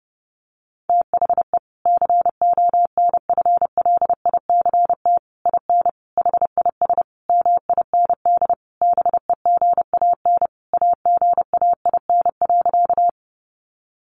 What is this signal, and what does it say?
Morse code, heard as a keyed tone, about 30 wpm, 700 Hz. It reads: THE CONFLICT IN HIS MIND BEGAN AGAIN.
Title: Sentences from Top 1300 Words 30wpm